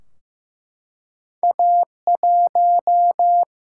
Morse code, heard as a keyed tone, about 15 wpm, 700 Hz. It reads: A 1
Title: Single-Letter Single-Number 15wpm